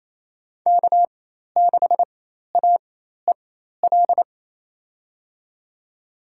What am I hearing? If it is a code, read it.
Morse code, heard as a keyed tone, about 28 wpm, spaced as for 13 wpm, 700 Hz. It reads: K6AEL
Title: USA Call Signs - 5 Characters - 4x Character Spacing - 28wpm